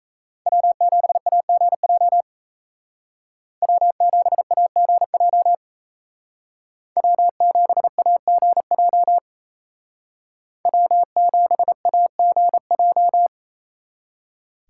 Morse code, heard as a keyed tone, about 36 wpm, 700 Hz. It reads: W7AGJ W7AGJ W7AGJ W7AGJ